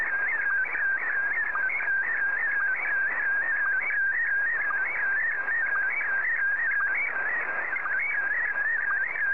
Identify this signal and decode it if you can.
SSTV (slow-scan television), Scottie DX.